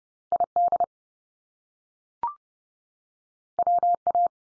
Morse code, heard as a keyed tone, about 30 wpm, 700 Hz. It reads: ID WA
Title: US States 30WPM